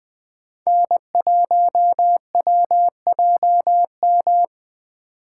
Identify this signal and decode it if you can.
Morse code, heard as a keyed tone, about 18 wpm, 700 Hz. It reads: N1WJM
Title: Speed Racing - USA Call Signs 15wpm